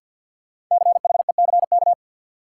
Morse code, heard as a keyed tone, about 53 wpm, 700 Hz. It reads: CHECK
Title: Sentences from Top 1000 Words - Review Part 3 - New Word Drill - 50wpm